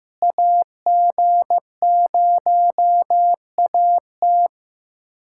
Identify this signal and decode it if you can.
Morse code, heard as a keyed tone, about 15 wpm, 700 Hz. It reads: AG0AT
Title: USA Callsign DE Callsign - 5 Characters - 15wpm